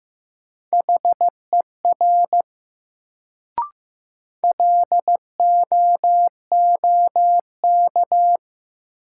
Morse code, heard as a keyed tone, about 15 wpm, 700 Hz. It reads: HER LOOK
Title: Sets of 2 Words - New Word Drill - 15wpm